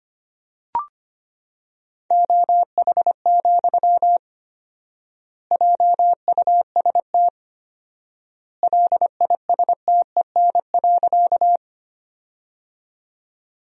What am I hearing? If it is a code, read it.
Morse code, heard as a keyed tone, about 25 wpm, 700 Hz. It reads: OH, JUST LISTEN.
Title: 4x Word-Spacing - Sentences from Top 500 Words - 25wpm